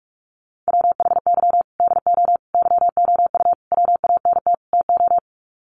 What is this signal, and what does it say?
Morse code, heard as a keyed tone, about 44 wpm, 700 Hz. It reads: WHY DO YOU WANT TO
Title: Sets of 5 Words Encore2 45wpm